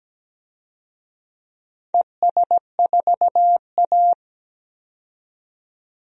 Morse code, heard as a keyed tone, about 17 wpm, 700 Hz. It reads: ES4A
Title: DXCC-100 Call Signs - 4 Characters - 17wpm